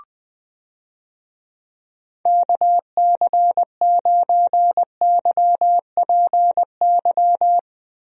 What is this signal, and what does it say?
Morse code, heard as a keyed tone, about 20 wpm, 700 Hz. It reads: KC9YPY
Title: USA Call Signs - 6 Characters - Repeated 2x - 20wpm